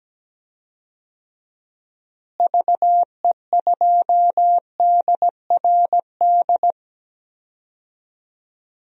Morse code, heard as a keyed tone, about 17 wpm, 700 Hz. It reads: VE2DRD
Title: DXCC-100 Call Signs - 6 Characters - Repeated 3x - 17wpm